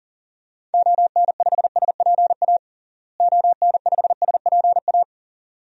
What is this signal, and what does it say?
Morse code, heard as a keyed tone, about 40 wpm, 700 Hz. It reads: ON5SPA ON5SPA
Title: DXCC-100 Call Signs - 6 Characters - Repeated 2x - 40wpm